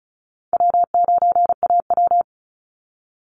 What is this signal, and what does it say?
Morse code, heard as a keyed tone, about 34 wpm, 700 Hz. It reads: W9AW
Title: Easy Call Signs Repeated 3x - 35wpm